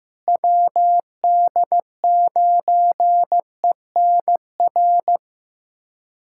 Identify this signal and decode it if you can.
Morse code, heard as a keyed tone, about 15 wpm, 700 Hz. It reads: WD9ENR
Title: USA Callsign DE Callsign - 6 Characters - 15wpm